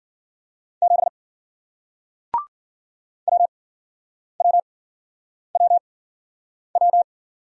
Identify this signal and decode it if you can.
Morse code, heard as a keyed tone, about 53 wpm, spaced as for 9 wpm, 700 Hz. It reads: 7 WWWW